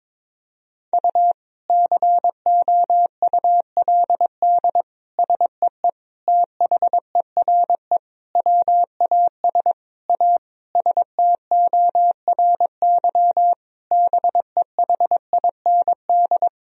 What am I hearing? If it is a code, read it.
Morse code, heard as a keyed tone, about 22 wpm, 700 Hz. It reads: U COULD SEE THERE WAS A STORY BEHIND
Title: Sentences from Top 500 Words and Abbreviations 22wpm